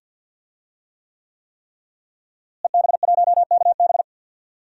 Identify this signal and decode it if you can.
Morse code, heard as a keyed tone, about 52 wpm, 700 Hz. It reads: EB1KB
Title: DXCC-100 Call Signs - 5 Characters - Repeated 3x - 50wpm